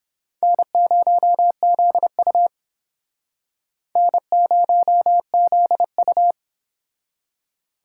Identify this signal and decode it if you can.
Morse code, heard as a keyed tone, about 30 wpm, 700 Hz. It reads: N0ZU N0ZU